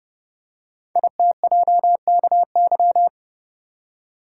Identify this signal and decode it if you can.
Morse code, heard as a keyed tone, about 29 wpm, 700 Hz. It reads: ITJKY
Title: Code Groups - 5 Letters - 30wpm